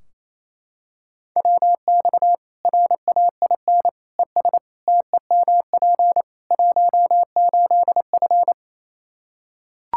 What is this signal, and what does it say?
Morse code, heard as a keyed tone, about 28 wpm, 700 Hz. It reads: WX RAIN ES TEMP 18F
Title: K1USN SST QSO Practice 28wpm